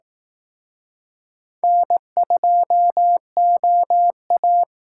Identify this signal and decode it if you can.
Morse code, heard as a keyed tone, about 18 wpm, 700 Hz. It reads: N2OA
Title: Speed Racing - USA 1x2 Call Signs 15wpm